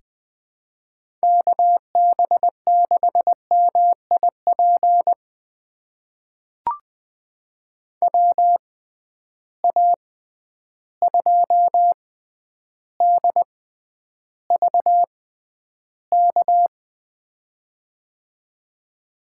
Morse code, heard as a keyed tone, about 20 wpm, spaced as for 10 wpm, 700 Hz. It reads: KB6MIP WA2DVK
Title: USA Call Signs - 6 Characters - 6x Character Spacing - 20wpm